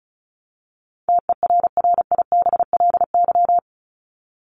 Morse code, heard as a keyed tone, about 34 wpm, 700 Hz. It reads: TERRIBLY
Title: Top 1751-2000 Words 35wpm